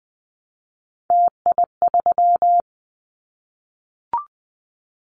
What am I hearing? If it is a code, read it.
Morse code, heard as a keyed tone, about 20 wpm, 700 Hz. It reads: TI3
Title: Callsign Prefixes for Top 100 DXCC - 20wpm